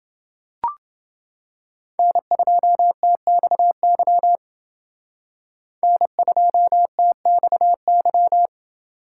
Morse code, heard as a keyed tone, about 28 wpm, 700 Hz. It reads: N2TXY N2TXY